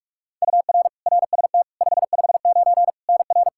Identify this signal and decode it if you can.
Morse code, heard as a keyed tone, about 46 wpm, 700 Hz. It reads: UR RST 559 NR
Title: Practice QSO - Exchange 1 - Beginning - 45wpm